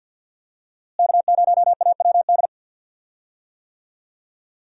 Morse code, heard as a keyed tone, about 51 wpm, 700 Hz. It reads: K0AWD
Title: USA Call Signs - 5 Characters - Repeated 2x - 50wpm